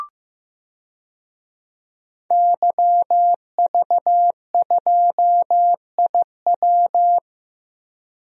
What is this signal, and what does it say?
Morse code, heard as a keyed tone, about 15 wpm, 700 Hz. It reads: YV2IW
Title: DXCC-100 Call Signs - 5 Characters - Repeated 2x - 15wpm